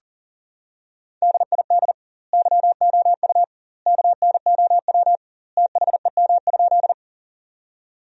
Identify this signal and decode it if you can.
Morse code, heard as a keyed tone, about 40 wpm, 700 Hz. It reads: DID YOU KNOW THEM?
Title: 2x Word-Spacing - Sentences from Top 100 Words - 40wpm